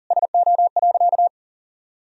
Morse code, heard as a keyed tone, about 40 wpm, 700 Hz. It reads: SO.
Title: Sentences from Top 1000 Words - Review Part 4 - 40wpm